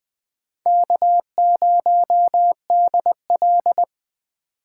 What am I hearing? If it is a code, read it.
Morse code, heard as a keyed tone, about 21 wpm, 700 Hz. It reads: K0DL